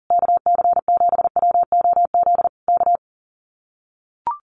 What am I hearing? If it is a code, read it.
Morse code, heard as a keyed tone, about 40 wpm, 700 Hz. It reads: KC7WOZ K
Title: Common QSO Phrases Encore 40wpm